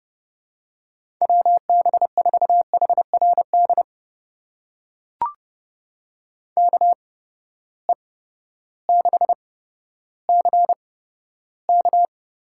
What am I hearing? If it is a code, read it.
Morse code, heard as a keyed tone, about 32 wpm, spaced as for 14 wpm, 700 Hz. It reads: WB4HRD KE6CK
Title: USA Call Signs - 6 Characters - 8x Character Spacing - 30wpm